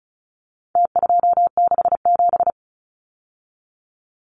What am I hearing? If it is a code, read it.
Morse code, heard as a keyed tone, about 34 wpm, 700 Hz. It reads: T267
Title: Numbers - Contest Style - 4-Digit 35wpm